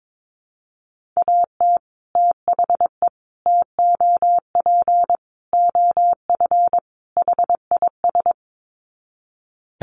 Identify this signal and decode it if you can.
Morse code, heard as a keyed tone, about 22 wpm, 700 Hz. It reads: AT THE TOP OF HIS
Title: Sets of 5 Words Encore 22wpm